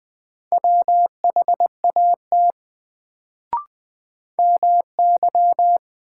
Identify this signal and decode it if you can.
Morse code, heard as a keyed tone, about 20 wpm, 700 Hz. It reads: WHAT MY
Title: Sets of 2 Words - New Word Drill - 20wpm